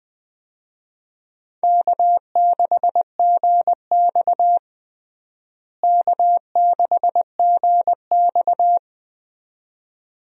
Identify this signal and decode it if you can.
Morse code, heard as a keyed tone, about 20 wpm, 700 Hz. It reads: K6GX K6GX
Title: USA Call Signs - 4 Characters - Repeated 2x - 20wpm